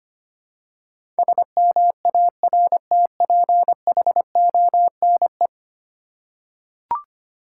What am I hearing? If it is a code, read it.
Morse code, heard as a keyed tone, about 25 wpm, 700 Hz. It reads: SMARTPHONE
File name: 10 Letter Words - First 3 Letters Given - 25wpm